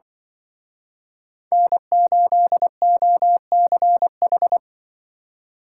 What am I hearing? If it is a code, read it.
Morse code, heard as a keyed tone, about 24 wpm, 700 Hz. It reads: N8OCH
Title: Speed Racing - USA 1x3 Call Signs 20wpm